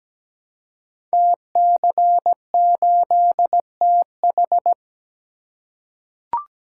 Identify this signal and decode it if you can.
Morse code, heard as a keyed tone, about 17 wpm, 700 Hz. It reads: TC8TH
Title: Code Groups - 5 Letter:Numbers - 17wpm